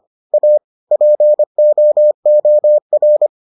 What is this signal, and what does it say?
Morse code, heard as a keyed tone, about 25 wpm, 600 Hz. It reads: A POOR